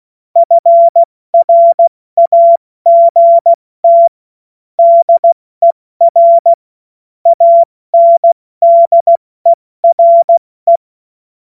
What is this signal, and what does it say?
Morse code, heard as a keyed tone, about 16 wpm, spaced as for 14 wpm, 680 Hz. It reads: FRAGT DER ANDERE
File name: Witz-22-Juli-2025-70-F80